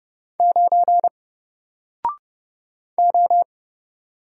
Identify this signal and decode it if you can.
Morse code, heard as a keyed tone, about 29 wpm, 700 Hz. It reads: 9 O
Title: Single Letter-Number Alt 30wpm